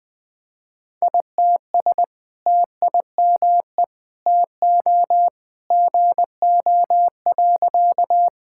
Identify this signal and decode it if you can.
Morse code, heard as a keyed tone, about 21 wpm, 700 Hz. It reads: ITS TIME TO GO.
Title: Sentences with 2-Word Phrases - 20wpm